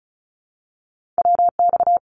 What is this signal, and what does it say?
Morse code, heard as a keyed tone, about 36 wpm, 700 Hz. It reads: WX
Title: Abbreviations 35wpm